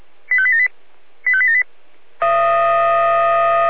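Five-tone selective call (ZVEI), heard as quick runs of five tones, tones of about 70 ms. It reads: 85878 85878
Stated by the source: ZVEI five-tone selective call received on an SDR.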